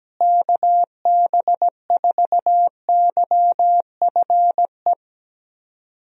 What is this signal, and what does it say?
Morse code, heard as a keyed tone, about 17 wpm, 700 Hz. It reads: KB4YFE
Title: USA Call Signs - 6 Characters - Repeated 2x - 17wpm